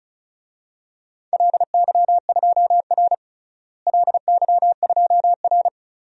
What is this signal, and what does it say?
Morse code, heard as a keyed tone, about 34 wpm, 700 Hz. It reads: LY2R LY2R
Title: DXCC-100 Call Signs - 4 Characters - Repeated 2x - 35wpm